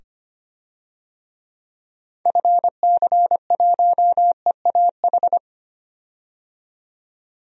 Morse code, heard as a keyed tone, about 25 wpm, 700 Hz. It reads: FC1EAH
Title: DXCC-100 Call Signs - 6 Characters - 25wpm